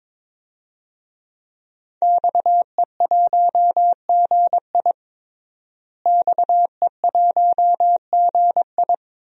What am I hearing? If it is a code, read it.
Morse code, heard as a keyed tone, about 22 wpm, 700 Hz. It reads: XE1GI XE1GI
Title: DXCC-100 Call Signs - 5 Characters - Repeated 2x - 22wpm